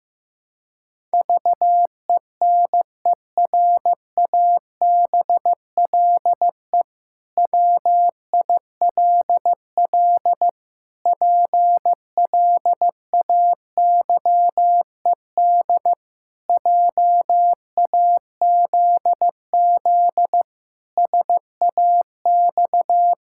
Morse code, heard as a keyed tone, about 16 wpm, 700 Hz. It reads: VENERABLE WILL PLAYED JAZZ SAX
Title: Pangram Challenge v2 - 15wpm